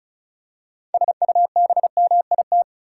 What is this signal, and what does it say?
Morse code, heard as a keyed tone, about 35 wpm, 700 Hz. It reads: SUBMIT
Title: Six Letter Words - 35wpm